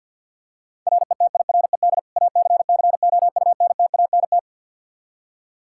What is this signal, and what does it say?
Morse code, heard as a keyed tone, about 51 wpm, 700 Hz. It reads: RETIRED ACCOUNTANT
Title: Practice QSO - Exchange 3 - More Info - 50wpm